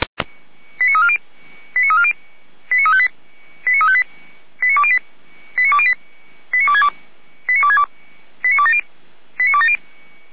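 Five-tone selective call (ZVEI), heard as quick runs of five tones, tones of about 70 ms. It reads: 89340 89340 89358 89358 89208 89262 89262 89270
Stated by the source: Five-tone ZVEI selective call received by SDR, from audio.f85360.feuerwehr-orlamuende.org (Feuerwehr Orlamünde fire brigade).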